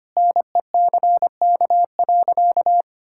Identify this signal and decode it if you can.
Morse code, heard as a keyed tone, about 25 wpm, 700 Hz. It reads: NECK.
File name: practice / 25wpm / Sentences with 4-Word Phrases - 25wpm